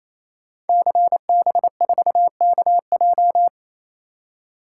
Morse code, heard as a keyed tone, about 29 wpm, 700 Hz. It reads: CB4KJ
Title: Code Groups - 5 Letter:Numbers - 28wpm